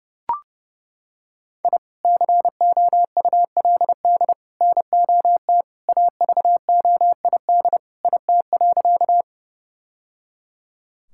Morse code, heard as a keyed tone, about 30 wpm, 700 Hz. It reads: I COULD NOT AVOID IT.
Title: Sentences from Top 1000 Words - Review Part 6 - 30wpm